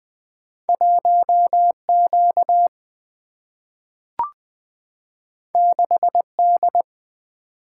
Morse code, heard as a keyed tone, about 20 wpm, 700 Hz. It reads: 1 Q 6 D
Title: Single-Letter Single-Number 20wpm